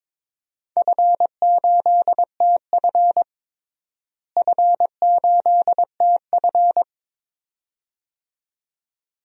Morse code, heard as a keyed tone, about 22 wpm, 700 Hz. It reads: F8TF F8TF
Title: DXCC-100 Call Signs - 4 Characters - Repeated 2x - 22wpm